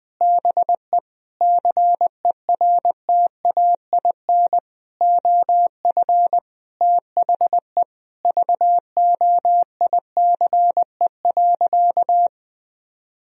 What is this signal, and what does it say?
Morse code, heard as a keyed tone, about 21 wpm, 700 Hz. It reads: BE CERTAIN OF THE VOICE.
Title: Sentences from Top 1000 Words - Review Part 2 - 20wpm